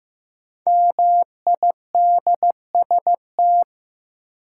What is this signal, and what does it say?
Morse code, heard as a keyed tone, about 15 wpm, 700 Hz. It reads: MIDST
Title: Sets of 4 Words - New Word Drill - 15wpm